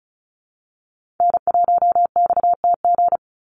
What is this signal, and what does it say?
Morse code, heard as a keyed tone, about 34 wpm, 700 Hz. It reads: N1XTG
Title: USA Call Signs 35wpm